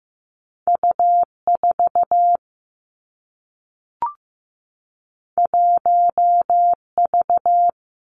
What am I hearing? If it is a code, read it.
Morse code, heard as a keyed tone, about 15 wpm, 700 Hz. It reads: U 4 1 V
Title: Two Challenging Letter-Number Combinations 15wpm